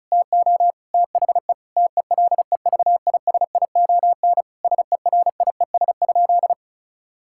Morse code, heard as a keyed tone, about 34 wpm, 700 Hz. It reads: TO THE TELEVISION SERIES?
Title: Sentences from Top 1000 Words - Review Part 5 - 35wpm